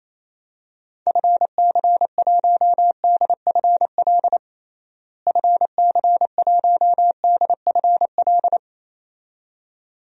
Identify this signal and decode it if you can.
Morse code, heard as a keyed tone, about 28 wpm, 700 Hz. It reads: FC1DFL FC1DFL
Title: DXCC-100 Call Signs - 6 Characters - Repeated 2x - 28wpm